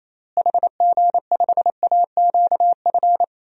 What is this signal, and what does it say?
Morse code, heard as a keyed tone, about 28 wpm, 700 Hz. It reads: HG5AQF
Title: DXCC-100 Call Signs - 6 Characters - Repeated 3x - 28wpm